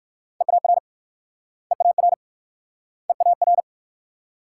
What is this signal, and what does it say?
Morse code, heard as a keyed tone, about 59 wpm, 690 Hz. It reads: EAR EAR EAR